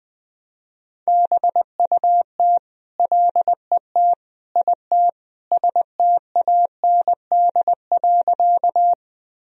Morse code, heard as a keyed tone, about 20 wpm, 700 Hz. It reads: BUT LET IT STAND.
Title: Sentences from Top 1000 Words - Review Part 1 - 20wpm